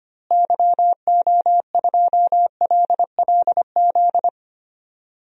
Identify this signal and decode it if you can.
Morse code, heard as a keyed tone, about 25 wpm, 700 Hz. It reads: YO2LLZ
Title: DXCC-100 Call Signs - 6 Characters - Repeated 2x - 25wpm